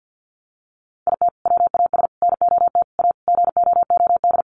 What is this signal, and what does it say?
Morse code, heard as a keyed tone, about 48 wpm, 700 Hz. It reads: IT WAS NOT A GOOD
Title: Sentences from Top 1100 Words 50wpm